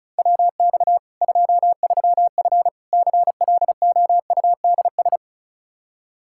Morse code, heard as a keyed tone, about 36 wpm, 700 Hz. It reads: WX 23F CLOUDS
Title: Practice QSO - Exchange 2 - Info - 35wpm